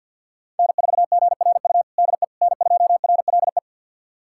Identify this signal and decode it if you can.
Morse code, heard as a keyed tone, about 52 wpm, 700 Hz. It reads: N4GRU DE N2RLE
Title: USA Callsign DE Callsign - 5 Characters - 50wpm